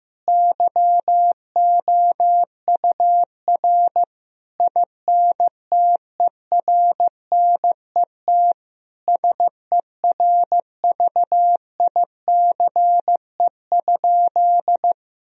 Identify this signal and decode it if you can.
Morse code, heard as a keyed tone, about 16 wpm, 700 Hz. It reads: YOUR INTERNET SERVICE?
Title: Sentences from Top 1000 Words - Review Part 4 - 15wpm